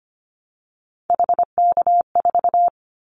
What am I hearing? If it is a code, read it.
Morse code, heard as a keyed tone, about 25 wpm, 700 Hz. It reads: HK4
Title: Callsign Prefixes for Top 100 DXCC - 25wpm